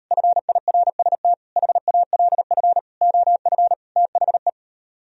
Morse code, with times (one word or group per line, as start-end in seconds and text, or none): 0.1-1.3 FIRST
1.6-2.8 HALF
3.0-3.7 OF
4.0-4.5 THE